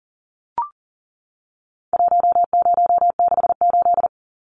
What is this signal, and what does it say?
Morse code, heard as a keyed tone, about 40 wpm, 700 Hz. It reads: 1068
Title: Numbers - 4-Digit 40wpm